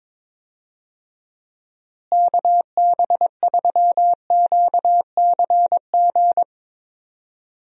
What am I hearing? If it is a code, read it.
Morse code, heard as a keyed tone, about 21 wpm, 700 Hz. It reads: KB3QCG
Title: USA Call Signs - 6 Characters - Repeated 3x - 22wpm